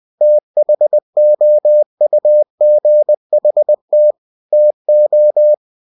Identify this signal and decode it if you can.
Morse code, heard as a keyed tone, about 20 wpm, 600 Hz. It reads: THOUGHT TO
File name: cw-books / winnie-the-pooh / winnie-20wpm-page-0091